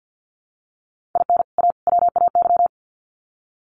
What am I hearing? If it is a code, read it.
Morse code, heard as a keyed tone, about 50 wpm, 700 Hz. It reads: IN A WAY
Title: Sets of 3 Words 50wpm